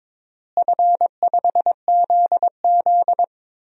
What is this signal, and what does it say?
Morse code, heard as a keyed tone, about 22 wpm, 700 Hz. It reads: F5ZZ
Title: DXCC-100 Call Signs - 4 Characters - Repeated 3x - 22wpm